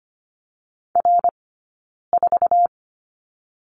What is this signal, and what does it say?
Morse code, heard as a keyed tone, about 25 wpm, 700 Hz. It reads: R 4
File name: course / 25:10wpm / 099 Lesson 17d - Letter u - Two Characters 25wpm